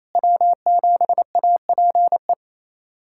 Morse code, heard as a keyed tone, about 28 wpm, 700 Hz. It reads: W7APE
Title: USA Callsign DE Callsign 28wpm